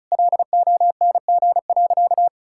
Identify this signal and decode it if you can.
Morse code, heard as a keyed tone, about 34 wpm, 700 Hz. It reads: LONG.
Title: Sentences from Top 1000 Words - Review Part 5 - 35wpm